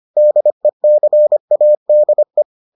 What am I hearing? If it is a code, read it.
Morse code, heard as a keyed tone, about 25 wpm, 600 Hz. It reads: DECADE